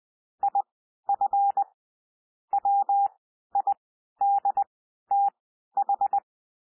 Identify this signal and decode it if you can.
Morse code, heard as a keyed tone, about 20 wpm, spaced as for 12 wpm, 800 Hz. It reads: IF WIDTH
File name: cwmp3 / P1 SOUND PASSAGE 20 wpm